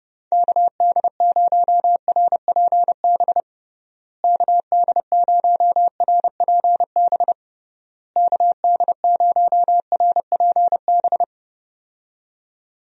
Morse code, heard as a keyed tone, about 30 wpm, 700 Hz. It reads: KD0RPB KD0RPB KD0RPB
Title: USA Call Signs - 6 Characters - Repeated 3x - 30wpm